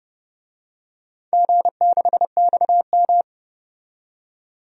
Morse code, heard as a keyed tone, about 30 wpm, 700 Hz. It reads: G6XM
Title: DXCC-100 Call Signs - 4 Characters - 30wpm